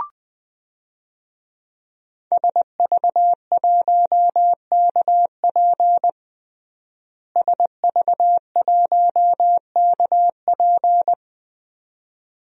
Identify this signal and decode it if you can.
Morse code, heard as a keyed tone, about 21 wpm, 700 Hz. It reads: SV1KP SV1KP
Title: DXCC-100 Call Signs - 5 Characters - Repeated 2x - 20wpm